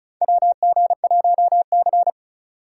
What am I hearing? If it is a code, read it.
Morse code, heard as a keyed tone, about 36 wpm, 700 Hz. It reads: WG1C
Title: USA Call Signs - 2x Character Spacing - 35wpm